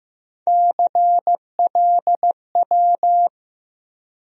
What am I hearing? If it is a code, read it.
Morse code, heard as a keyed tone, about 15 wpm, 700 Hz. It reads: CLW
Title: USA Call Sign Postfixes - 3 Letters 15wpm